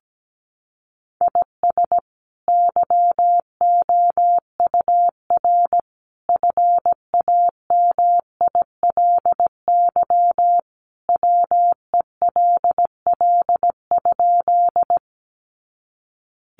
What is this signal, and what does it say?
Morse code, heard as a keyed tone, about 17 wpm, 700 Hz. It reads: IS YOUR FAMILY WELL?
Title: Sentences from Top 400 Words 17wpm